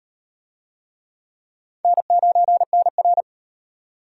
Morse code, heard as a keyed tone, about 34 wpm, 700 Hz. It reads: N9NR